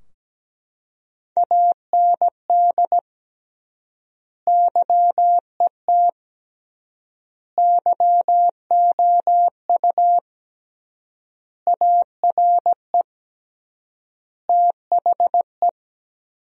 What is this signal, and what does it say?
Morse code, heard as a keyed tone, about 17 wpm, 700 Hz. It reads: AND YET YOU ARE THE
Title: 3x Word-Spacing - Sentences from Top 500 Words - 17wpm